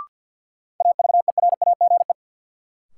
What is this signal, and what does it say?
Morse code, heard as a keyed tone, about 53 wpm, 700 Hz. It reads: AVERAGE
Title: Sentences from Top 1000 Words - Review Part 6 - New Word Drill - 50wpm